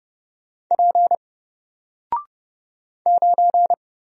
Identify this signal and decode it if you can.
Morse code, heard as a keyed tone, about 30 wpm, 700 Hz. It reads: P 9
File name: Single-Letter-Number-30wpm